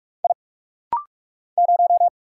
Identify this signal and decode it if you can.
Morse code, heard as a keyed tone, about 46 wpm, 700 Hz. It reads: I 0
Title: Single Letter-Number Alt - Rapid-Fire 45wpm